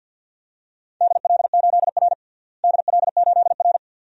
Morse code, heard as a keyed tone, about 51 wpm, 700 Hz. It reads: DL8R DL8R
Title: DXCC-100 Call Signs - 4 Characters - Repeated 2x - 50wpm